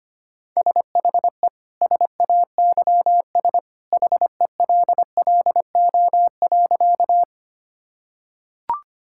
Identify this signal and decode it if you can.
Morse code, heard as a keyed tone, about 25 wpm, 690 Hz. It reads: SHE SAYS HELLO.
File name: practice / 25wpm / Sentences with 2-Word Phrases - 25wpm